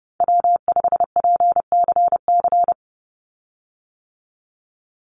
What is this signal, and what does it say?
Morse code, heard as a keyed tone, about 30 wpm, 700 Hz. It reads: W5PCC
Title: USA Call Signs 30wpm